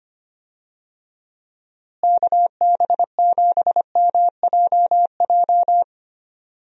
Morse code, heard as a keyed tone, about 25 wpm, 700 Hz. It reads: KB7MJJ
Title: USA Call Signs - 6 Characters - Repeated 3x - 25wpm